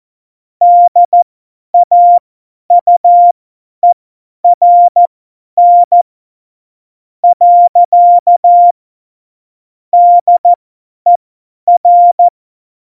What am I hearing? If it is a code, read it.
Morse code, heard as a keyed tone, about 14 wpm, spaced as for 10 wpm, 700 Hz. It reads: DAUERN . DER
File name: Morsekurs / Witze / Witz 15. Mai 2018, 50 F70